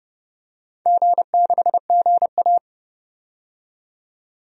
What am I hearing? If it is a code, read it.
Morse code, heard as a keyed tone, about 30 wpm, 700 Hz. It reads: G6GA
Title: DXCC-100 Call Signs - 4 Characters - 30wpm